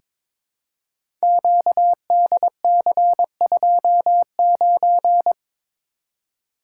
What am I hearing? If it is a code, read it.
Morse code, heard as a keyed tone, about 23 wpm, 700 Hz. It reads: QDC29
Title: Code Groups - 5 Letter:Numbers - 22wpm